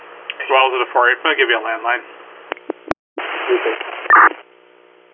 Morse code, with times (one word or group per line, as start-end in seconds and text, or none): none